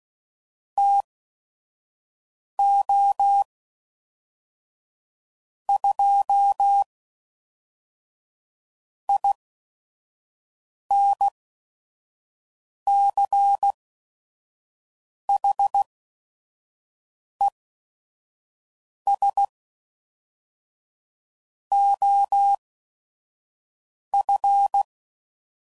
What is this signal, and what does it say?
Morse code, heard as a keyed tone, about 16 wpm, spaced as for 5 wpm, 800 Hz. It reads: TO 2 INCHES OF
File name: CW Code Groups @5WPM / Weather Sets / Weather 3 17